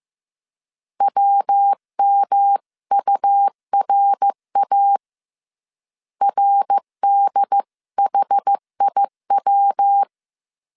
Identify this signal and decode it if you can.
Morse code, heard as a keyed tone, about 15 wpm, 800 Hz. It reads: WMURA RDHIW